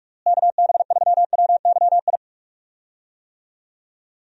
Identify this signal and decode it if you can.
Morse code, heard as a keyed tone, about 46 wpm, 700 Hz. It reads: KB3WYI
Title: USA Call Signs - 6 Characters - Repeated 3x - 45wpm